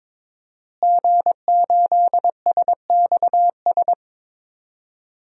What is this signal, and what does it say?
Morse code, heard as a keyed tone, about 22 wpm, 700 Hz. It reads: G8SXS
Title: DXCC-100 Call Signs - 5 Characters - Repeated 3x - 22wpm